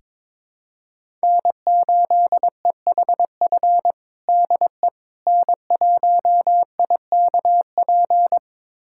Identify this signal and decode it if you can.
Morse code, heard as a keyed tone, about 22 wpm, 700 Hz. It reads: N8EHF DE N1IKP
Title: USA Callsign DE Callsign - 5 Characters - 22wpm